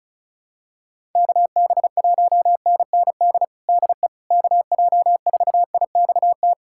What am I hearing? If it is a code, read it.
Morse code, heard as a keyed tone, about 36 wpm, 700 Hz. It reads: KB1NND DE KJ4IXT
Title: USA Callsign DE Callsign - 6 Characters - 35wpm